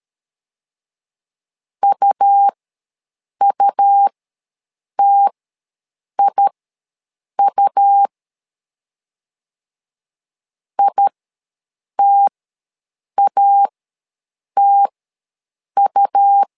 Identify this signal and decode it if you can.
Morse code, heard as a keyed tone, about 13 wpm, spaced as for 7 wpm, 800 Hz. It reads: UUTIU ITATU
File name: LEZIONI / 2 UHDN / 2 U ws 12wpm 1655